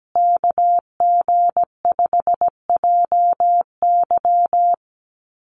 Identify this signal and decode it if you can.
Morse code, heard as a keyed tone, about 17 wpm, 700 Hz. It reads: KG5JY
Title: USA Call Signs Repeated 3x 17wpm